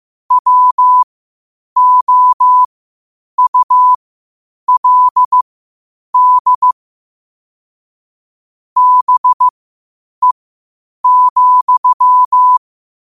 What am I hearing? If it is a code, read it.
Morse code, heard as a keyed tone, about 15 wpm, spaced as for 8 wpm, 1000 Hz. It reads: WOULD BE,